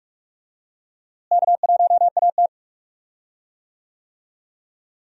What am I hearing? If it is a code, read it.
Morse code, heard as a keyed tone, about 46 wpm, 700 Hz. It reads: K1AT
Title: USA Call Signs - 4 Characters - Repeated 3x - 45wpm